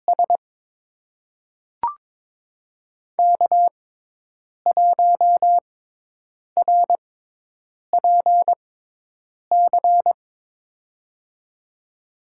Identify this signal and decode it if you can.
Morse code, heard as a keyed tone, about 22 wpm, spaced as for 8 wpm, 700 Hz. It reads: S K1RPC
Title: USA Call Signs - 5 Characters - 6x Character Spacing - 22wpm